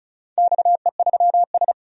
Morse code, heard as a keyed tone, about 36 wpm, 700 Hz. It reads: XE3S
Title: DXCC-100 Call Signs - 4 Characters - 35wpm